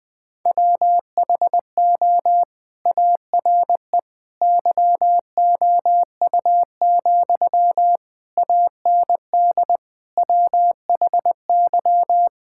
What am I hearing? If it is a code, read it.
Morse code, heard as a keyed tone, about 20 wpm, 700 Hz. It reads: WHO ARE YOU, AND WHY
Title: Sentences from Top 1600 Words 20wpm